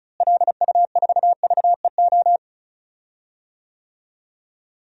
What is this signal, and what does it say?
Morse code, heard as a keyed tone, about 34 wpm, 700 Hz. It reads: LU4VEO
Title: DXCC-100 Call Signs - 6 Characters - Repeated 2x - 35wpm